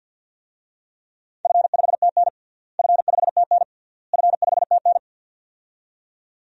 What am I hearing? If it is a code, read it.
Morse code, heard as a keyed tone, about 50 wpm, 700 Hz. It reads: F5TN F5TN F5TN
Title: DXCC-100 Call Signs - 4 Characters - Repeated 3x - 50wpm